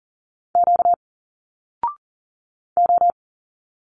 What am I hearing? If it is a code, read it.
Morse code, heard as a keyed tone, about 41 wpm, 700 Hz. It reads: Q O
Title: Single Letters 40wpm